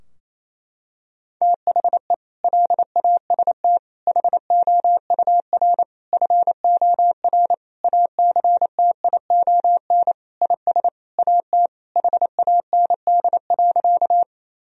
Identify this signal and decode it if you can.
Morse code, heard as a keyed tone, about 28 wpm, 700 Hz. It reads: THE LAST HOUR FOR ACTION IS AT HAND.
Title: Sentences from Top 1000 Words - Review Part 4 - 28wpm